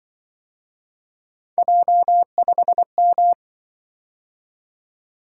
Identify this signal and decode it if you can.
Morse code, heard as a keyed tone, about 24 wpm, 700 Hz. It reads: J5M
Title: Speed-Racing - Three Letter-Number Combinations - 20wpm